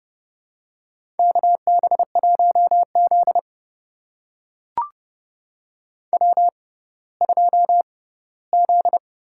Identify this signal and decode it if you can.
Morse code, heard as a keyed tone, about 32 wpm, spaced as for 17 wpm, 700 Hz. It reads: KB1Z W2Z
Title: USA Call Signs - 4 Characters - 6x Character Spacing - 30wpm